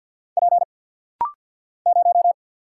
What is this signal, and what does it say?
Morse code, heard as a keyed tone, about 49 wpm, 700 Hz. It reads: P 0
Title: Single Letter-Number - Rapid-Fire 50wpm